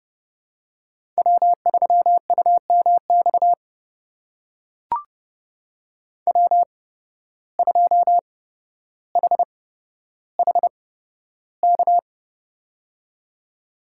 Morse code, heard as a keyed tone, about 30 wpm, spaced as for 13 wpm, 700 Hz. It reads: W3UMX W2HHK
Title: USA Call Signs - 5 Characters - 8x Character Spacing - 30wpm